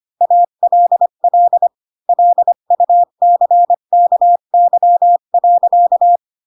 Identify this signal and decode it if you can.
Morse code, heard as a keyed tone, about 25 wpm, 700 Hz. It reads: ALL LUCKY.
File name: Pinocchio / Pinocchio ch3 20 25 ik1hge